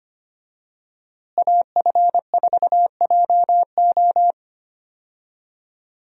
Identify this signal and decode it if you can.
Morse code, heard as a keyed tone, about 25 wpm, 700 Hz. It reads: AF4JO
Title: USA Call Signs - 5 Characters - Repeated 3x - 25wpm